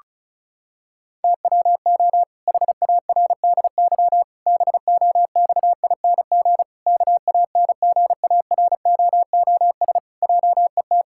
Morse code, heard as a keyed tone, about 36 wpm, 700 Hz. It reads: TWO HARDY BOXING KANGAROOS JET
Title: Pangram Challenge v2 - 35wpm